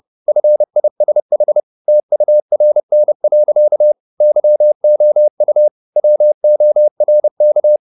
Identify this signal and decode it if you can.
Morse code, heard as a keyed tone, about 30 wpm, 600 Hz. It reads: FISH TURN. YOU WORK